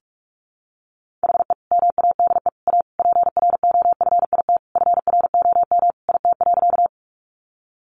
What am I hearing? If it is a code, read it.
Morse code, heard as a keyed tone, about 44 wpm, 700 Hz. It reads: HE MADE A PROFIT FROM IT.
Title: Sentences from Top 800 Words 45wpm